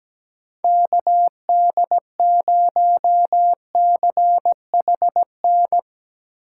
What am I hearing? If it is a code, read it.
Morse code, heard as a keyed tone, about 19 wpm, 700 Hz. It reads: KD0CHN
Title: Speed Racing - USA 2x3 Call Signs 17wpm